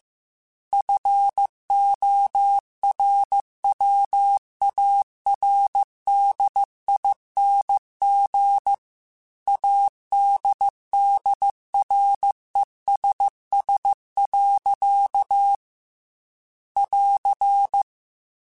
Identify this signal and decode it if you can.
Morse code, heard as a keyed tone, about 15 wpm, 800 Hz. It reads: FORWARDING ADDRESS. <AR>